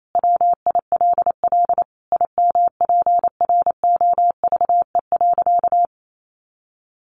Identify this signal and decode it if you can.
Morse code, heard as a keyed tone, about 28 wpm, 700 Hz. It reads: WILL IMPROVE.
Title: Sentences from Top 800 Words 28wpm